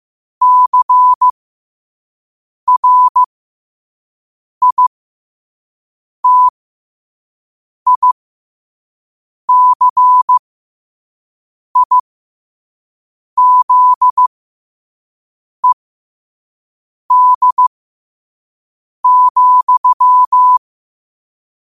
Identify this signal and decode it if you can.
Morse code, heard as a keyed tone, about 15 wpm, spaced as for 5 wpm, 1000 Hz. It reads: CRITICIZED,
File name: qotdfiles / 5wpm